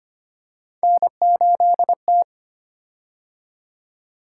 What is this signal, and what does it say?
Morse code, heard as a keyed tone, about 25 wpm, 700 Hz. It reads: N8T
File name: Three Letter-Number Combinations - 25wpm